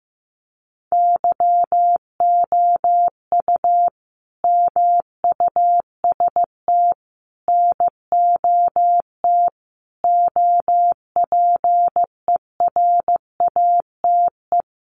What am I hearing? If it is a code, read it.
Morse code, heard as a keyed tone, about 15 wpm, 700 Hz. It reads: YOU MUST NOT OPERATE
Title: Sentences from Top 1500 Words 15wpm